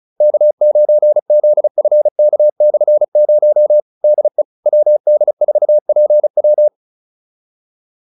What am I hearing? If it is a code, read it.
Morse code, heard as a keyed tone, about 35 wpm, 600 Hz. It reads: K9ZFK/0 DE WD4PW
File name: QSO303 35